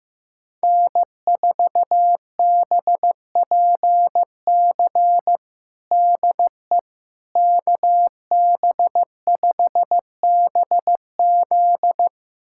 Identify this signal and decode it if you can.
Morse code, heard as a keyed tone, about 16 wpm, 700 Hz. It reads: N4BPC DE KB5BZ
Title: USA Callsign DE Callsign - 5 Characters - 15wpm